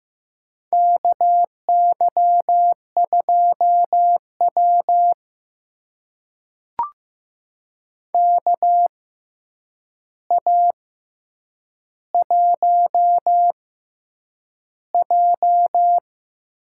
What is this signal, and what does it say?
Morse code, heard as a keyed tone, about 15 wpm, spaced as for 8 wpm, 700 Hz. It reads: KY2W KA1J
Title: USA Call Signs - 6x Character Spacing - 15wpm